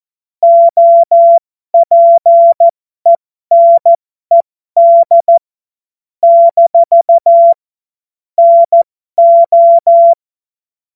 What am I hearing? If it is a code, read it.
Morse code, heard as a keyed tone, about 14 wpm, spaced as for 12 wpm, 680 Hz. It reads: OPENED - NO